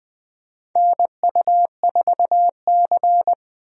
Morse code, heard as a keyed tone, about 20 wpm, 700 Hz. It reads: NU4C
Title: USA Call Signs - 4 Characters - 20wpm